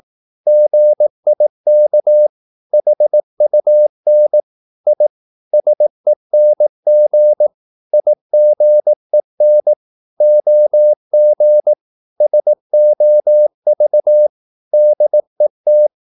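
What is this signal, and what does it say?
Morse code, heard as a keyed tone, about 18 wpm, 600 Hz. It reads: GIK HUN I SENG IGEN OG SOV DET
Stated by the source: Slow (QRS) Morse practice at 18 wpm, CW med Gnister.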